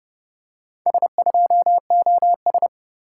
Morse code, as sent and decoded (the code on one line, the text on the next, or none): ... ..--- --- ...
S2OS